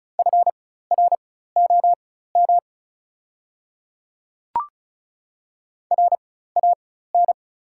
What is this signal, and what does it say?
Morse code, heard as a keyed tone, about 34 wpm, spaced as for 16 wpm, 700 Hz. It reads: FROM RAN
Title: 4x Character Spacing - Top 500 Words - 35wpm